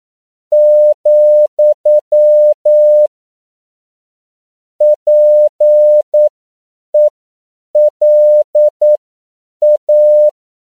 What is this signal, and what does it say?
Morse code, heard as a keyed tone, about 9 wpm, spaced as for 7 wpm, 600 Hz. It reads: , PELA